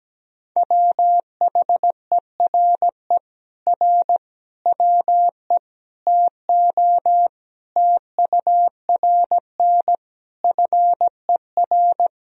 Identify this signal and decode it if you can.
Morse code, heard as a keyed tone, about 17 wpm, 700 Hz. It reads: WHERE R WE TO TURN FER
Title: Sentences from Top 200 Words and Abbreviations 17wpm